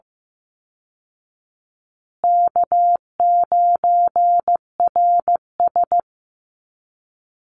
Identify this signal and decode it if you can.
Morse code, heard as a keyed tone, about 15 wpm, 700 Hz. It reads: K9RS
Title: Easy Call Signs Repeated 3x - 15wpm